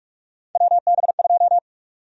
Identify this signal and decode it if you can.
Morse code, heard as a keyed tone, about 46 wpm, 700 Hz. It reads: WB2
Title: USA Call Sign Prefixes - Repeated 2x - 45wpm